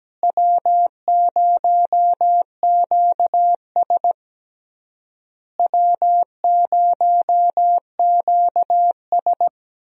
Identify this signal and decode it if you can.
Morse code, heard as a keyed tone, about 17 wpm, 700 Hz. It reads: W0QS W0QS
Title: USA Call Signs - 4 Characters - Repeated 3x - 17wpm